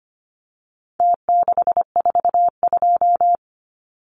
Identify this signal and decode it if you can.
Morse code, heard as a keyed tone, about 25 wpm, 700 Hz. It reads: T642
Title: Numbers - Contest Style - 4-Digit 25wpm